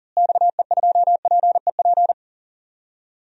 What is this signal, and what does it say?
Morse code, heard as a keyed tone, about 40 wpm, 700 Hz. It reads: XE2PEP
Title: DXCC-100 Call Signs - 6 Characters - Repeated 2x - 40wpm